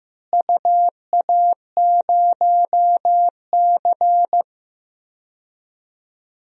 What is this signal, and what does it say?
Morse code, heard as a keyed tone, about 15 wpm, 700 Hz. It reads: UA0C
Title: DXCC-100 Call Signs - 4 Characters - Repeated 3x - 15wpm